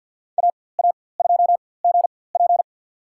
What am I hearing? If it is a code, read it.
Morse code, heard as a keyed tone, about 49 wpm, spaced as for 23 wpm, 700 Hz. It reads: AA2GP
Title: USA Call Signs - 4x Character Spacing - 50wpm